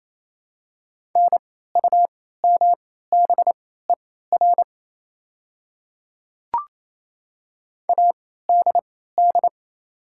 Morse code, heard as a keyed tone, about 28 wpm, spaced as for 16 wpm, 700 Hz. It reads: NUMBER ADD
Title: 3x Character Spacing - Top 200 Words - 28wpm